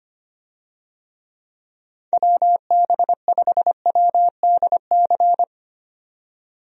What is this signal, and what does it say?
Morse code, heard as a keyed tone, about 25 wpm, 700 Hz. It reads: WB5WDC
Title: USA Call Signs - 6 Characters - 25wpm